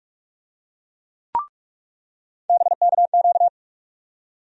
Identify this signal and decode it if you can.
Morse code, heard as a keyed tone, about 44 wpm, 700 Hz. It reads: BKQ
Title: USA Call Sign Postfixes - 3 Letters 45wpm